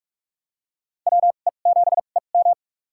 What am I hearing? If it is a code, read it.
Morse code, heard as a keyed tone, about 47 wpm, spaced as for 33 wpm, 700 Hz. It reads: WE7EM
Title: USA Call Signs - 5 Characters - 2x Character Spacing - 45wpm